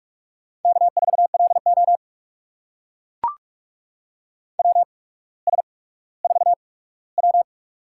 Morse code, heard as a keyed tone, about 44 wpm, spaced as for 19 wpm, 700 Hz. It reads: K4LO WS4W
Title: USA Call Signs - 4 Characters - 8x Character Spacing - 45wpm